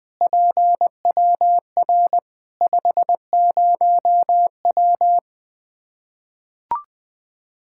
Morse code, heard as a keyed tone, about 20 wpm, 700 Hz. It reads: PWR 50W
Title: K1USN SST QSO Practice 20wpm